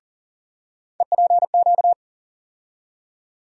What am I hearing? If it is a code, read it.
Morse code, heard as a keyed tone, about 40 wpm, 700 Hz. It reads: EPQ
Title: USA Call Sign Postfixes - 3 Letters 40wpm